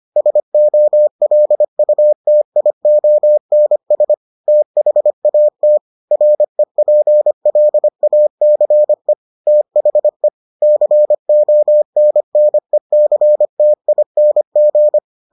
Morse code, heard as a keyed tone, about 25 wpm, 600 Hz. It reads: SOLUTIONS THAT REPLACE THE CONNECTING